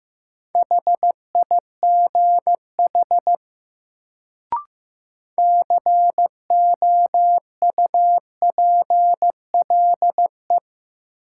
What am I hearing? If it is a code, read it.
Morse code, heard as a keyed tone, about 15 wpm, 700 Hz. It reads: HIGH COUPLE
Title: Sets of 2 Words - New Word Drill - 15wpm